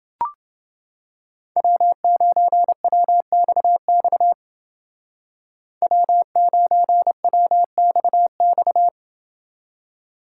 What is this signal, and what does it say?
Morse code, heard as a keyed tone, about 29 wpm, 700 Hz. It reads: W9WXX W9WXX